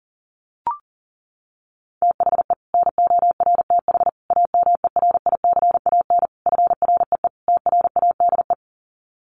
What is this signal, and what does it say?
Morse code, heard as a keyed tone, about 37 wpm, 700 Hz. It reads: THE NORTH AMERICAN FREE TRADE
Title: Sets of 5 Words Encore 40wpm